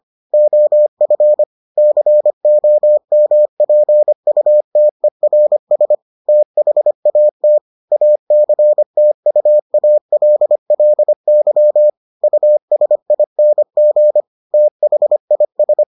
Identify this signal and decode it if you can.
Morse code, heard as a keyed tone, about 25 wpm, 600 Hz. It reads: OF COMPUTERS THAT ACTUALLY USING THIS